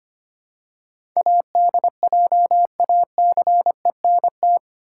Morse code, heard as a keyed tone, about 25 wpm, 700 Hz. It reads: ADJACENT
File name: 8 Letter Words - First 3 Letters Given - 25wpm